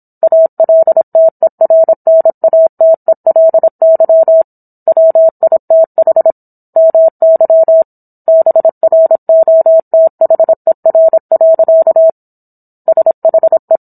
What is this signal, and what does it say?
Morse code, heard as a keyed tone, about 27 wpm, 650 Hz. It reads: ALTERNATELY WITH MY BROTHER. SHE